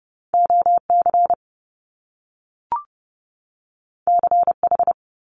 Morse code, heard as a keyed tone, about 30 wpm, 700 Hz. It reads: OC CH
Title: Top Two Letter Combinations in Words 30wpm